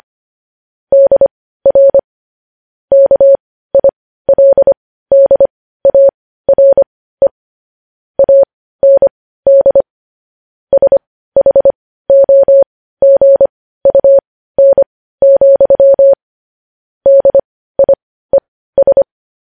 Morse code, heard as a keyed tone, about 25 wpm, spaced as for 15 wpm, 550 Hz. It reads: DR KILDARE AND SHOGUN, DIES
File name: podcast / News headlines-1x-25x15